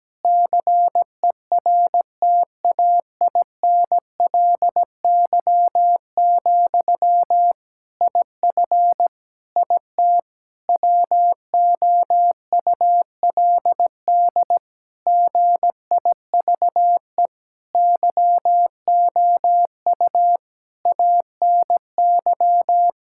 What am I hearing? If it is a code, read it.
Morse code, heard as a keyed tone, about 17 wpm, 700 Hz. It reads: CERTAINLY, IF IT WOULD GIVE YOU ANY
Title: Sentences from Top 1600 Words 17wpm